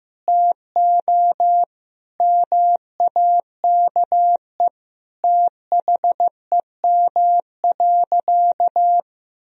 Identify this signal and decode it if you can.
Morse code, heard as a keyed tone, about 16 wpm, 700 Hz. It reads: TO MAKE THEM.
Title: Sentences from Top 1000 Words - Review Part 1 - 15wpm